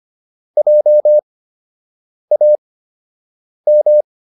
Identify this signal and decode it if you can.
Morse code, heard as a keyed tone, about 25 wpm, spaced as for 7 wpm, 600 Hz. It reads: JAM